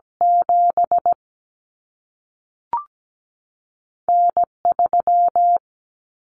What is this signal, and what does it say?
Morse code, heard as a keyed tone, about 17 wpm, spaced as for 3 wpm, 700 Hz. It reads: 7N 3
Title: Common Cut Numbers - 2-Digit 17wpm